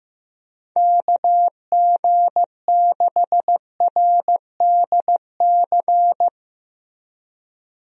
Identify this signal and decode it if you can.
Morse code, heard as a keyed tone, about 15 wpm, 700 Hz. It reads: KG6RDC
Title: USA Call Signs - 6 Characters - Repeated 2x - 15wpm